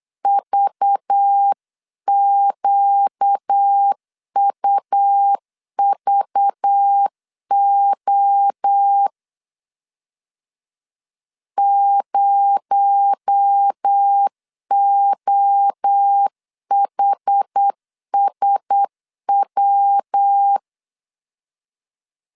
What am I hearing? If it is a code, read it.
Morse code, heard as a keyed tone, about 8 wpm, 800 Hz. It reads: VQUVO 0OHSW
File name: CW / DET / 1 6 1500C S B 8wpm 5142